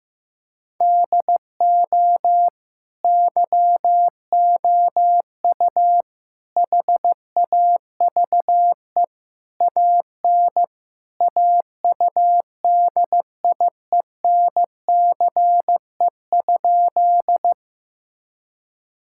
Morse code, heard as a keyed tone, about 15 wpm, 700 Hz. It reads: DO YOU HAVE AN AUDIENCE?
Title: Sentences from Top 1000 Words - Review Part 5 - 15wpm